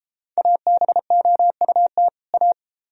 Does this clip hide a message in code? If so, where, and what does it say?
Yes: Morse code from 0.4 to 2.5 s, heard as a keyed tone, about 31 wpm, 700 Hz. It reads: ABOUT A